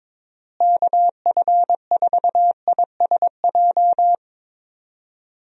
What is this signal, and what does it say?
Morse code, heard as a keyed tone, about 21 wpm, 700 Hz. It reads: KF4ISJ
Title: USA Call Signs - 6 Characters - Repeated 3x - 22wpm